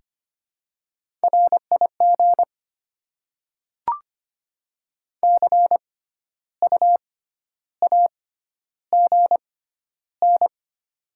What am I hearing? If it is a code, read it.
Morse code, heard as a keyed tone, about 25 wpm, spaced as for 12 wpm, 700 Hz. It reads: RIG CU AGN
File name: Top 20 QSO Elements - 6x Character Spacing - 25wpm